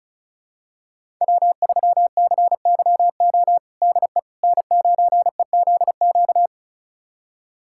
Morse code, heard as a keyed tone, about 36 wpm, 700 Hz. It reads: W3CYO DE N9EZQ
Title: USA Callsign DE Callsign - 5 Characters - 35wpm